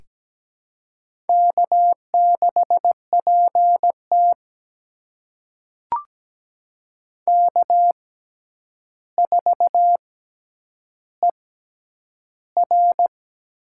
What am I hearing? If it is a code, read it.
Morse code, heard as a keyed tone, about 17 wpm, spaced as for 9 wpm, 700 Hz. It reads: K6PT K4ER
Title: USA Call Signs - 4 Characters - 6x Character Spacing - 17wpm